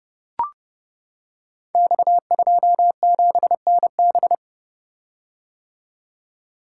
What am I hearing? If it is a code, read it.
Morse code, heard as a keyed tone, about 30 wpm, 700 Hz. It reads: X27NB
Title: Code Groups - 5 Letter:Numbers - 30wpm